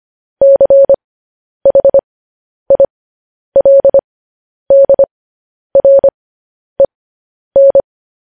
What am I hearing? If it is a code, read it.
Morse code, heard as a keyed tone, about 25 wpm, spaced as for 10 wpm, 550 Hz. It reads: CHILDREN
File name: podcast / News headlines-1x-25x10